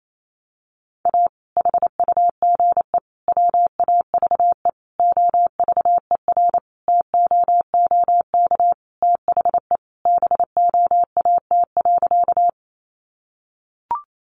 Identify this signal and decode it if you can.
Morse code, heard as a keyed tone, about 28 wpm, 700 Hz. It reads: A HUGE WAVE OVERTOOK THE BOAT.
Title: Sentences from Top 1400 Words 28wpm